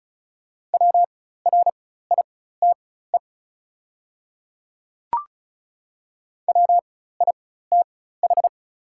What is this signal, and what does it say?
Morse code, heard as a keyed tone, about 36 wpm, spaced as for 16 wpm, 700 Hz. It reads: WRITE WITH
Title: 4x Character Spacing - Top 100 Words - 35wpm